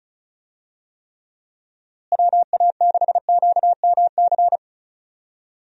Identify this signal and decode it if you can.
Morse code, heard as a keyed tone, about 36 wpm, 700 Hz. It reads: WA6QMC
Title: USA Call Signs - 6 Characters - Repeated 2x - 35wpm